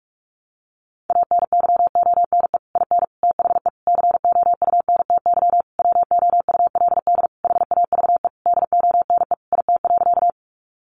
Morse code, heard as a keyed tone, about 44 wpm, 700 Hz. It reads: ANYONE IN THE COUNTY WOULD HAVE DONE IT.
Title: Sentences from Top 1200 Words 45wpm